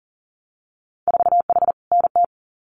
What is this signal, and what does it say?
Morse code, heard as a keyed tone, about 42 wpm, 700 Hz. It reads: 4H NT
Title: Winter Field Day Exchanges Encore 40wpm